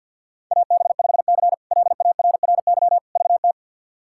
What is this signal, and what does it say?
Morse code, heard as a keyed tone, about 52 wpm, 700 Hz. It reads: AB5C LARRY VT
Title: K1USN SST QSO Practice - 50wpm